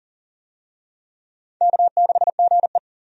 Morse code, heard as a keyed tone, about 40 wpm, 700 Hz. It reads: K6GE
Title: USA Call Signs - 4 Characters - 40wpm